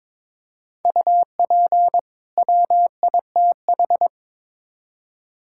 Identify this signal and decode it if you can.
Morse code, heard as a keyed tone, about 22 wpm, 700 Hz. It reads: UP WITH
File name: Build A Phrase - 2-Word Phrases - 22wpm